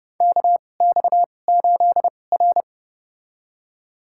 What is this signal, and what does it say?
Morse code, heard as a keyed tone, about 30 wpm, spaced as for 22 wpm, 700 Hz. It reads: KX8R
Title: USA Call Signs - 4 Characters - 2x Character Spacing - 30wpm